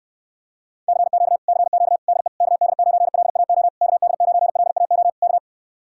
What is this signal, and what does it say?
Morse code, heard as a keyed tone, about 69 wpm, 700 Hz. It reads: CQ CQ DE KD9LAO KD9LAO K